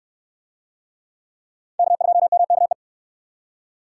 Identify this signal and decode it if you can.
Morse code, heard as a keyed tone, about 65 wpm, 700 Hz. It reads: K2MKE